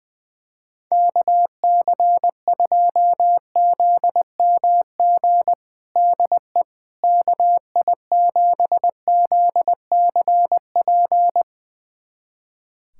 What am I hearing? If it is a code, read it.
Morse code, heard as a keyed tone, about 20 wpm, 700 Hz. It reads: KC2ZMG DE KI7ZCP
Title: USA Callsign DE Callsign - 6 Characters - 20wpm